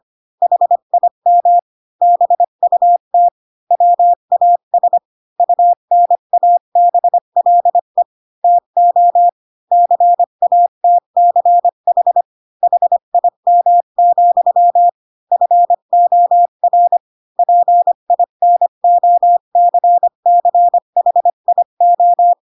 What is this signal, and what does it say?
Morse code, heard as a keyed tone, about 25 wpm, 700 Hz. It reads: HIM BUT WAS UNABLE TO CATCH HIM, FOR PINOCCHIO
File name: Pinocchio / Pinocchio ch3 20 25 ik1hge